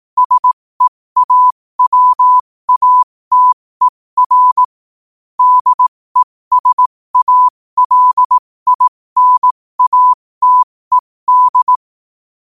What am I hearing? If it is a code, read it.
Morse code, heard as a keyed tone, about 18 wpm, spaced as for 15 wpm, 1000 Hz. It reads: SEAWATER DESALINATED
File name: qotdfiles / 15wpm